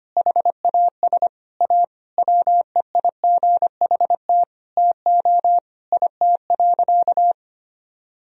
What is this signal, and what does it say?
Morse code, heard as a keyed tone, about 25 wpm, 700 Hz. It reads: HAS A WEIGHT TO IT.
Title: Sentences from Top 1000 Words - Review Part 3 - 25wpm